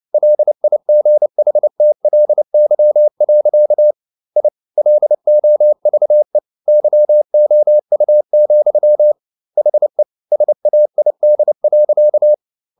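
Morse code, heard as a keyed tone, about 29 wpm, 600 Hz. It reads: LIGHTLY. I LOVE YOU, HE SAID.
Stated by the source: Morse code at 29 wpm, sending Ken Follett's novel Never in English, from CW med Gnister.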